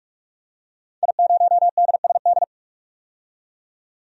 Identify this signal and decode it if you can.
Morse code, heard as a keyed tone, about 46 wpm, 700 Hz. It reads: I0DSD
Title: DXCC-100 Call Signs - 5 Characters - 45wpm